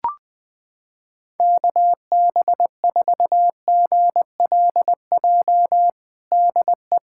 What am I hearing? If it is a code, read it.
Morse code, heard as a keyed tone, about 20 wpm, 700 Hz. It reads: KB4GLJ DE
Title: USA Callsign DE Callsign - 6 Characters - 20wpm